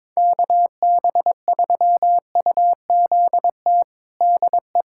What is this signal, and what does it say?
Morse code, heard as a keyed tone, about 22 wpm, 700 Hz. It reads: KB3UZT DE
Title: USA Callsign DE Callsign 22wpm